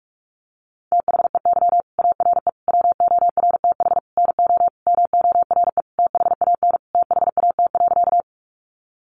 Morse code, heard as a keyed tone, about 44 wpm, 700 Hz. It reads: THEY ARE WORTH NO MORE THAN THAT.
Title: Sentences from Top 800 Words 45wpm